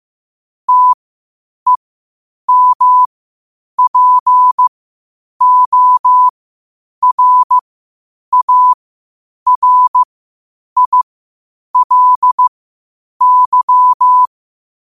Morse code, heard as a keyed tone, about 15 wpm, spaced as for 8 wpm, 1000 Hz. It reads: TEMPORARILY
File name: qotdfiles / 8wpm